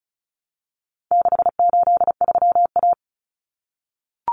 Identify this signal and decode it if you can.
Morse code, heard as a keyed tone, about 34 wpm, 700 Hz. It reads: 683A
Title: Common Cut Numbers - 4-Digit 35wpm